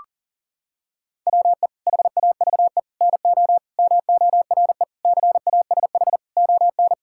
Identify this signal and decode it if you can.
Morse code, heard as a keyed tone, about 40 wpm, 700 Hz. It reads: WE HAVE NO MORE CASH ON
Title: Sentences from Top 1000 Words - Review Part 5 - 40wpm